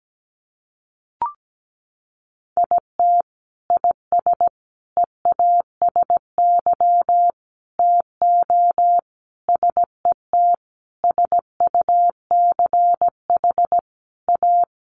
Morse code, heard as a keyed tone, about 17 wpm, 700 Hz. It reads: IT IS EASY TO SET SUCH A
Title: Sentences from Top 1300 Words 17wpm